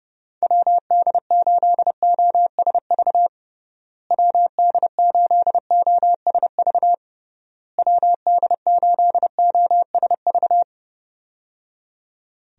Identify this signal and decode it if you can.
Morse code, heard as a keyed tone, about 30 wpm, 700 Hz. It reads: WD8OSV WD8OSV WD8OSV
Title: USA Call Signs - 6 Characters - Repeated 3x - 30wpm